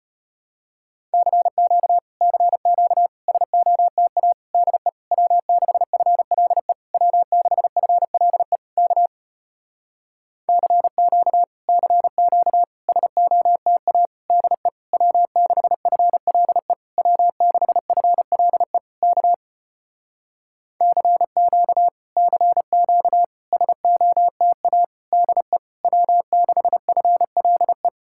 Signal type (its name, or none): Morse code